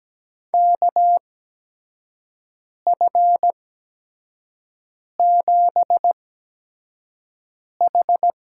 Morse code, heard as a keyed tone, about 17 wpm, spaced as for 5 wpm, 700 Hz. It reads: KF7H
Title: USA Call Signs - 6 Characters - 8x Character Spacing - 17wpm